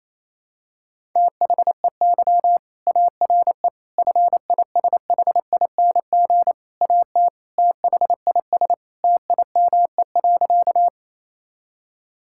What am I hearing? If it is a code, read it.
Morse code, heard as a keyed tone, about 28 wpm, 700 Hz. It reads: THEY ARE FISHING AT THIS TIME.
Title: Sentences from Top 1000 Words - Review Part 5 - 28wpm